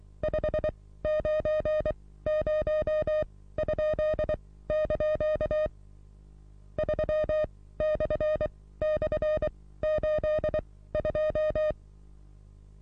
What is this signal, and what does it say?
Morse code, heard as a keyed tone, about 23 wpm, spaced as for 16 wpm, 620 Hz. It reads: 590?) 3//82
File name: repos / Morsekurs / DJ4UF Morsen27